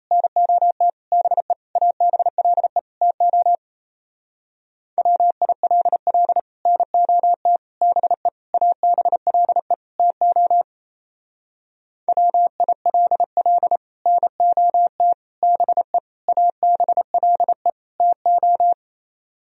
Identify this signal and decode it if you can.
Morse code, heard as a keyed tone, about 31 wpm, 700 Hz. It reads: NOT BE ABLE TO WILL NOT BE ABLE TO WILL NOT BE ABLE TO
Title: Speed-Racing - Sets of 5 Words 28wpm